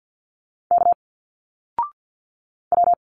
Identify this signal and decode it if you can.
Morse code, heard as a keyed tone, about 51 wpm, 700 Hz. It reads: K W
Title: Single Letters 50wpm